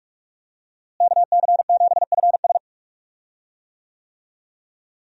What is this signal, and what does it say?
Morse code, heard as a keyed tone, about 45 wpm, 700 Hz. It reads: KC7FS
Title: USA Call Signs - 5 Characters - Repeated 3x - 45wpm